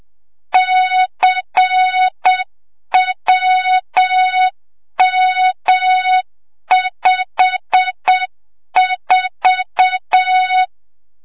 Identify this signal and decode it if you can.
Morse code, heard as a keyed tone, about 7 wpm, 3700 Hz, with noise. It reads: CWM54
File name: Morze-HA2MN / hang / 25 GYAKORLAT